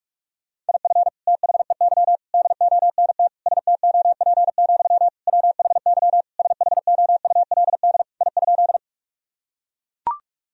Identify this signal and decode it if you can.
Morse code, heard as a keyed tone, about 42 wpm, 700 Hz. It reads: IF THEY DONT STOP, WHY SHOULD I?
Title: Sentences from Top 1000 Words - Review Part 1 - 45wpm